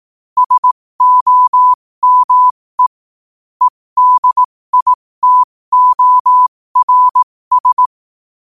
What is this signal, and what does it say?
Morse code, heard as a keyed tone, about 18 wpm, spaced as for 15 wpm, 1000 Hz. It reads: SOME EDITORS
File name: qotdfiles / 15wpm